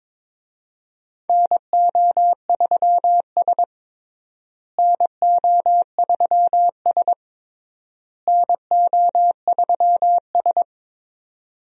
Morse code, heard as a keyed tone, about 23 wpm, 700 Hz. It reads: NO3S NO3S NO3S
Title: USA Call Signs - 4 Characters - Repeated 3x - 22wpm